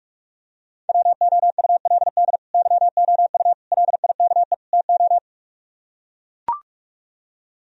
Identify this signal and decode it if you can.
Morse code, heard as a keyed tone, about 44 wpm, 690 Hz. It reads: WOULD YOU LIKE TO